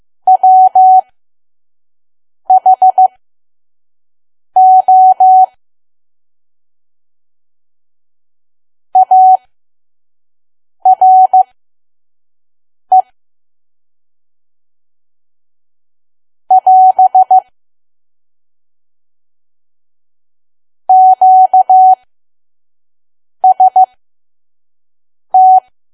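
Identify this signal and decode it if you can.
Morse code, heard as a keyed tone, about 15 wpm, spaced as for 5 wpm, 750 Hz. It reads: WHO ARE <AS> QST
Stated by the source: Morse code practice at 5 WPM from KQ2P.COM.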